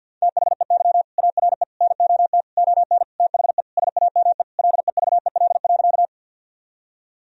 Morse code, heard as a keyed tone, about 52 wpm, 700 Hz. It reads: THEY ARE NOT ON THE SAME LEVEL.
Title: Sentences from Top 1000 Words - Review Part 4 - 50wpm